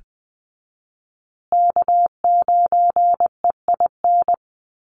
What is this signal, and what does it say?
Morse code, heard as a keyed tone, about 20 wpm, 700 Hz. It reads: K9EIN
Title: USA Call Signs Repeated 2x 20wpm